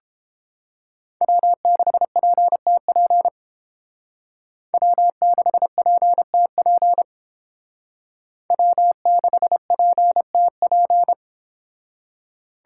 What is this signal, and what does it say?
Morse code, heard as a keyed tone, about 28 wpm, 700 Hz. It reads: W6PTP W6PTP W6PTP